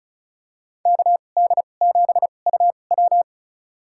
Morse code, heard as a keyed tone, about 36 wpm, spaced as for 25 wpm, 700 Hz. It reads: KD7UW
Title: USA Call Signs - 5 Characters - 2x Character Spacing - 35wpm